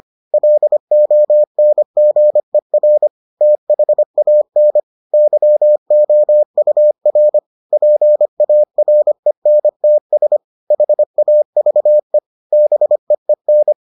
Morse code, heard as a keyed tone, about 25 wpm, 600 Hz. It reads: LONGER THAN YOUR PARENTS HAVE BEEN